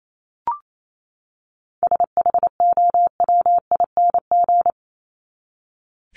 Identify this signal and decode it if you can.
Morse code, heard as a keyed tone, about 28 wpm, 700 Hz. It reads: SHOWING
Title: ING Suffix - Top 100 Words with Base Words 28wpm